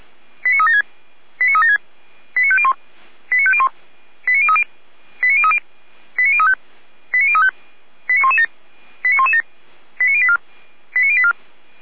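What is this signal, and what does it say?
Five-tone selective call (ZVEI), heard as quick runs of five tones, tones of about 80 ms. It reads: 89376 89376 89501 89501 89030 89030 89035 89035 89117 89117 89084 89084
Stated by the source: ZVEI five-tone call received on an SDR.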